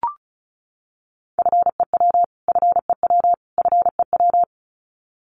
Morse code, heard as a keyed tone, about 34 wpm, 700 Hz. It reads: FEW FEW FEW
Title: Top 500 Words Repeated 35wpm